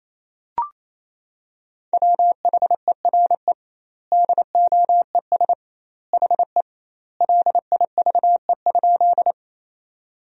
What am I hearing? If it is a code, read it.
Morse code, heard as a keyed tone, about 28 wpm, 700 Hz. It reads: WHERE DOES HE LIVE?
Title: 2x Word-Spacing - Sentences from Top 200 Words - 28wpm